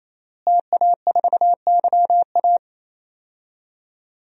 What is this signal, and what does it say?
Morse code, heard as a keyed tone, about 27 wpm, 700 Hz. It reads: TA4YA
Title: DXCC-100 Call Signs - 5 Characters - Repeated 3x - 28wpm